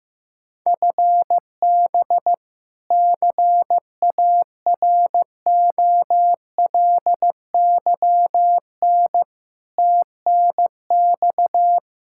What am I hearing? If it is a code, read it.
Morse code, heard as a keyed tone, about 16 wpm, 700 Hz. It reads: FB CAROLYN TNX
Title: Practice QSO - Exchange 3 - More Info - 15wpm